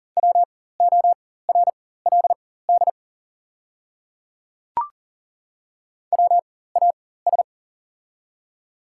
Morse code, heard as a keyed tone, about 37 wpm, spaced as for 18 wpm, 700 Hz. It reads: WORLD WAS
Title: 4x Character Spacing - Top 300 Words - 40wpm